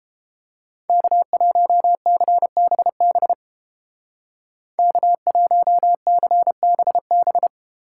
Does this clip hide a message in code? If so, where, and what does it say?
Yes: Morse code from 0.9 to 7.5 s, heard as a keyed tone, about 31 wpm, 700 Hz. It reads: K1CBB K1CBB